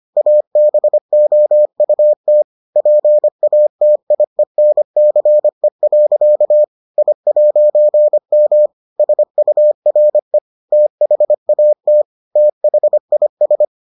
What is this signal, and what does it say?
Morse code, heard as a keyed tone, about 25 wpm, 600 Hz. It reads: ABOUT PATIENCE. I'M SURE THAT THIS